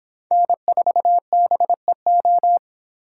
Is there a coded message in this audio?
Yes: Morse code from 0.3 to 2.6 s, heard as a keyed tone, about 30 wpm, 700 Hz. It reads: N4BEO